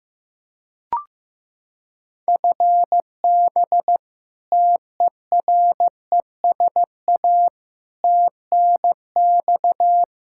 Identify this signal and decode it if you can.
Morse code, heard as a keyed tone, about 15 wpm, 700 Hz. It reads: FB TERESA TNX
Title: Practice QSO - Exchange 3 - More Info - 15wpm